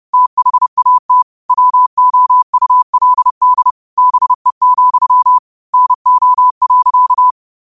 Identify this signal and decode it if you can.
Morse code, heard as a keyed tone, about 30 wpm, 1000 Hz. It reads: THAT WOULD BE, NO.
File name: qotdfiles / 30wpm